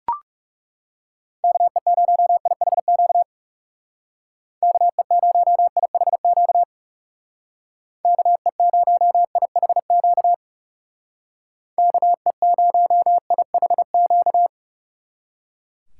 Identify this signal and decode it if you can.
Morse code, heard as a keyed tone, about 37 wpm, 700 Hz. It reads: KE0IHQ KE0IHQ KE0IHQ KE0IHQ
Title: Speed Racing - USA 2x3 Call Signs 30wpm